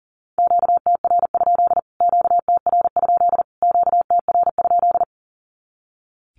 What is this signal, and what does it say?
Morse code, heard as a keyed tone, about 40 wpm, 700 Hz. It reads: QTR? QTR? QTR?
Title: Q-Codes Repeated 3x 40wpm